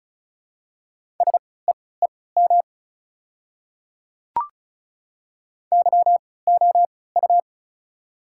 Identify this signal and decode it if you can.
Morse code, heard as a keyed tone, about 34 wpm, spaced as for 20 wpm, 700 Hz. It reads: SEEM YOU
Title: 3x Character Spacing - Top 300 Words - 35wpm